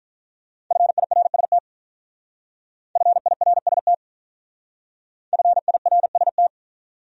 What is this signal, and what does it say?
Morse code, heard as a keyed tone, about 44 wpm, 700 Hz. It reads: FIRST FIRST FIRST
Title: Speed-Racing Top 100 Words 35wpm